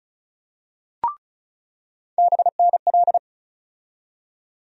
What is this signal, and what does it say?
Morse code, heard as a keyed tone, about 36 wpm, 700 Hz. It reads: BNL
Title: USA Call Sign Postfixes - 3 Letters 35wpm